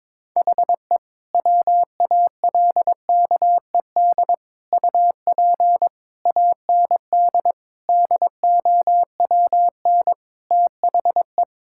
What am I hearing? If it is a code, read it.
Morse code, heard as a keyed tone, about 22 wpm, 700 Hz. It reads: HE WALKED UP AND DOWN THE
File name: practice / 22wpm / Sentences with 4-Word Phrases - 22wpm